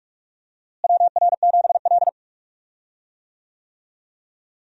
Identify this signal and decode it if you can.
Morse code, heard as a keyed tone, about 43 wpm, 700 Hz. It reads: WR7L
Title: USA Call Signs - 4 Characters - Repeated 3x - 45wpm